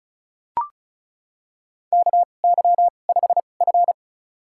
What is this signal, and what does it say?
Morse code, heard as a keyed tone, about 34 wpm, spaced as for 25 wpm, 700 Hz. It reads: KY5F
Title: USA Call Signs - 4 Characters - 2x Character Spacing - 35wpm